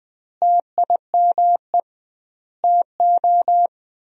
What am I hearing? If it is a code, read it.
Morse code, heard as a keyed tone, about 20 wpm, 700 Hz. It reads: TIME TO
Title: 2x Word-Spacing - Sentences from Top 300 Words - 20wpm